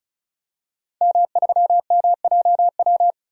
Morse code, heard as a keyed tone, about 34 wpm, 700 Hz. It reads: M3MJW
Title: DXCC-100 Call Signs - 5 Characters - 35wpm